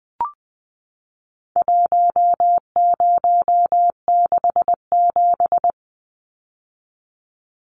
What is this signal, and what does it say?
Morse code, heard as a keyed tone, about 20 wpm, 700 Hz. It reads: 1067
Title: Numbers - 4-Digit 20wpm